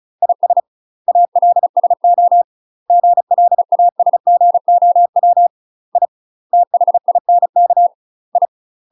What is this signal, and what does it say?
Morse code, heard as a keyed tone, about 34 wpm, 700 Hz. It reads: IS ALSO GLASGOW I THINK I
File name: UK MASTER QSOs TRAINING / 35WPM / UK MASTER QSOs 0025